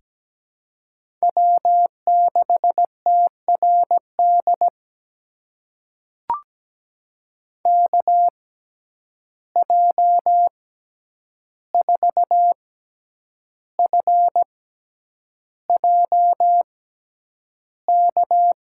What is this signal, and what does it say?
Morse code, heard as a keyed tone, about 17 wpm, spaced as for 8 wpm, 700 Hz. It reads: W6TRD KJ4FJK
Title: USA Call Signs - 6x Character Spacing - 17wpm